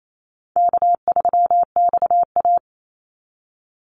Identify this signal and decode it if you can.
Morse code, heard as a keyed tone, about 28 wpm, 700 Hz. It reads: K3XA
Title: Easy Call Signs - 28wpm